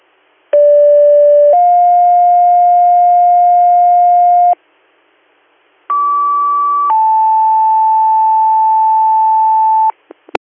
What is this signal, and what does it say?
Two-tone page, heard as a one-second tone, then a three-second tone, about 580 then 710 Hz, about 1150 then 880 Hz. It is heard from 0.5 to 9.9 s.